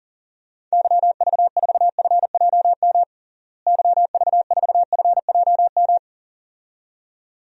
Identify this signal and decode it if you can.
Morse code, heard as a keyed tone, about 37 wpm, 700 Hz. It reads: YV4FJM YV4FJM
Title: DXCC-100 Call Signs - 6 Characters - Repeated 2x - 40wpm